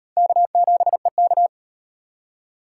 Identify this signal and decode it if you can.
Morse code, heard as a keyed tone, about 38 wpm, 700 Hz. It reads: K7EK